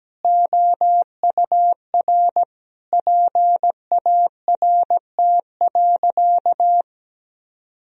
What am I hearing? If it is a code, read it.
Morse code, heard as a keyed tone, about 17 wpm, 700 Hz. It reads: OUR PART.
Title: Sentences from Top 1000 Words - Review Part 1 - 17wpm